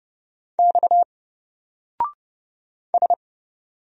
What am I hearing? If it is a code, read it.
Morse code, heard as a keyed tone, about 30 wpm, 700 Hz. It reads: X S
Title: Single Character (Letter:Number:Punctuation:Prosign) 30wpm